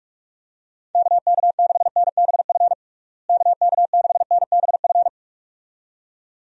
Morse code, heard as a keyed tone, about 46 wpm, 700 Hz. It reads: KK6NBF KK6NBF
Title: USA Call Signs - 6 Characters - Repeated 2x - 45wpm